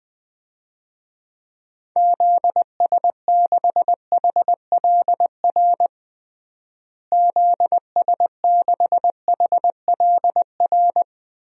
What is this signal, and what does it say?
Morse code, heard as a keyed tone, about 21 wpm, 700 Hz. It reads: ZS6HLR ZS6HLR
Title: DXCC-100 Call Signs - 6 Characters - Repeated 3x - 20wpm